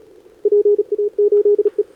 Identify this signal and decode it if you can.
Morse code, heard as a keyed tone, about 40 wpm, 410 Hz. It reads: PA8E
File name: CQWWCW2025 / NedeleRano 20m VK9DX Scan